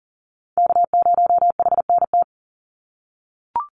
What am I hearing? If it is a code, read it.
Morse code, heard as a keyed tone, about 40 wpm, 700 Hz. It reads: K0HNT
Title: USA Call Signs 40wpm